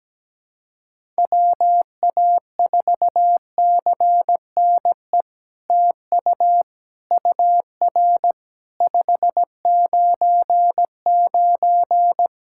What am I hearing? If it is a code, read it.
Morse code, heard as a keyed tone, about 17 wpm, 700 Hz. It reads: WA4CNE TU UR 599
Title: SOTA QSO Practice 17wpm